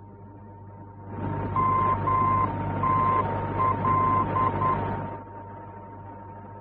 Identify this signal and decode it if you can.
Morse code, heard as a keyed tone, about 10 wpm, 1040 Hz, with noise. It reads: MTL
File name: france / mx / MTL-F-VOR